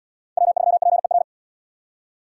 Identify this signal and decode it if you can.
Morse code, heard as a keyed tone, about 70 wpm, 700 Hz. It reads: W3PER